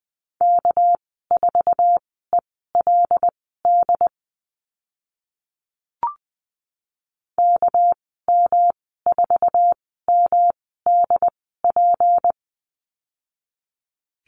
Morse code, heard as a keyed tone, about 20 wpm, spaced as for 14 wpm, 700 Hz. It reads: K4ELD KM4MDP
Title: 245 Lesson 39g - Letter z - Call Signs 20wpm